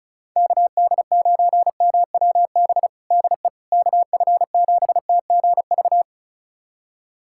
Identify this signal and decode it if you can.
Morse code, heard as a keyed tone, about 36 wpm, 700 Hz. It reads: KD9MWB DE KF7TGV
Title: USA Callsign DE Callsign - 6 Characters - 35wpm